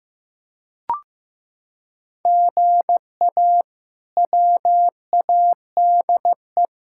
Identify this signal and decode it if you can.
Morse code, heard as a keyed tone, about 15 wpm, 700 Hz. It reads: GA WADE
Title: K1USN SST QSO Practice - 15wpm